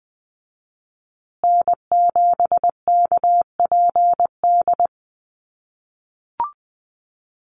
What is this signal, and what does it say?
Morse code, heard as a keyed tone, about 20 wpm, 700 Hz. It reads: N7KPD
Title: USA Call Signs Repeated 3x 20wpm